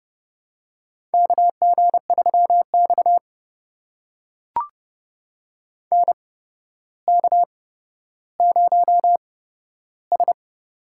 Morse code, heard as a keyed tone, about 30 wpm, spaced as for 13 wpm, 700 Hz. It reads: KG3X NK0S
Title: USA Call Signs - 4 Characters - 8x Character Spacing - 30wpm